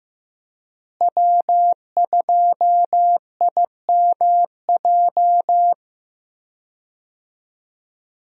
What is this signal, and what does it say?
Morse code, heard as a keyed tone, about 15 wpm, 700 Hz. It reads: W2IMJ
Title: USA Call Signs - 5 Characters - Repeated 3x - 15wpm